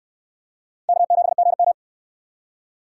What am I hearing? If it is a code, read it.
Morse code, heard as a keyed tone, about 68 wpm, 700 Hz. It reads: K7GM